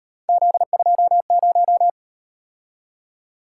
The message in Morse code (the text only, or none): Z20